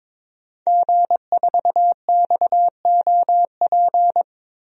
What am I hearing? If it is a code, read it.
Morse code, heard as a keyed tone, about 22 wpm, 700 Hz. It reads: G4XOP
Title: DXCC-100 Call Signs - 5 Characters - Repeated 2x - 22wpm